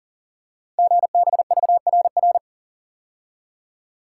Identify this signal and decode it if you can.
Morse code, heard as a keyed tone, about 43 wpm, 700 Hz. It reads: GBVRR
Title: Code Groups - 5 Letters - 40wpm